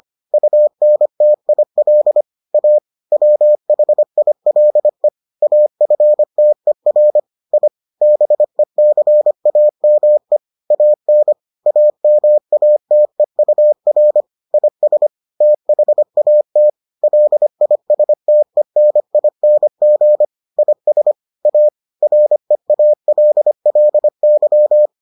Morse code, heard as a keyed tone, about 25 wpm, 600 Hz. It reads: UNTIL A WHILE AFTER I BECAME AN AMATEUR IS THAT LISTENING IS A REALLY